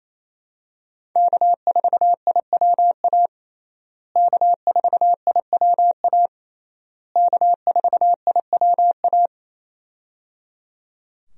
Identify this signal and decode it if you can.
Morse code, heard as a keyed tone, about 28 wpm, 700 Hz. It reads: K4IWA K4IWA K4IWA
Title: USA Call Signs - 5 Characters - Repeated 3x - 28wpm